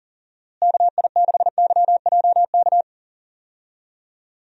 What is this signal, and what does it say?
Morse code, heard as a keyed tone, about 38 wpm, 700 Hz. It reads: KI6YJK
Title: USA Call Signs - 6 Characters - Repeated 3x - 40wpm